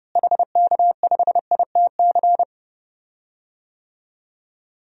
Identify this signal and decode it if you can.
Morse code, heard as a keyed tone, about 30 wpm, 700 Hz. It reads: HK5ITC
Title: DXCC-100 Call Signs - 6 Characters - Repeated 2x - 30wpm